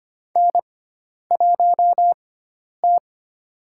Morse code, heard as a keyed tone, about 25 wpm, 700 Hz. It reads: N 1 T
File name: Lesson 16g - Letter c - Call Signs 25wpm